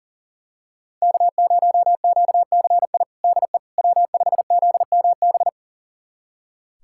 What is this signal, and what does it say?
Morse code, heard as a keyed tone, about 40 wpm, 700 Hz. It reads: K0QCI DE W5ZMB
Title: USA Callsign DE Callsign - 5 Characters - 40wpm